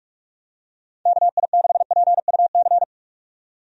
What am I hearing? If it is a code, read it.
Morse code, heard as a keyed tone, about 42 wpm, 700 Hz. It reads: KI6PUC
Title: USA Call Signs - 6 Characters - 45wpm